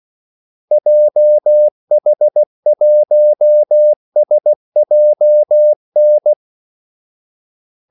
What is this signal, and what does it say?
Morse code, heard as a keyed tone, about 16 wpm, 600 Hz. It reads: JH1SJN
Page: Kaldesignaler 16 wpm